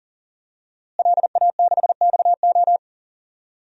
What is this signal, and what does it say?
Morse code, heard as a keyed tone, about 40 wpm, 700 Hz. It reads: LA6XO
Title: DXCC-100 Call Signs - 5 Characters - Repeated 3x - 40wpm